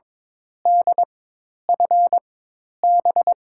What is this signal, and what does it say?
Morse code, heard as a keyed tone, about 21 wpm, spaced as for 10 wpm, 700 Hz. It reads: DFB
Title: USA Call Signs - 6 Characters - 4x Character Spacing - 22wpm